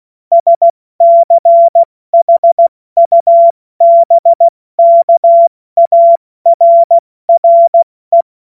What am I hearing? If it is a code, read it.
Morse code, heard as a keyed tone, about 16 wpm, 680 Hz. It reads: SCHUBKARRE
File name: Morsekurs / Witze / Witz 27 Februar 2024 70 F80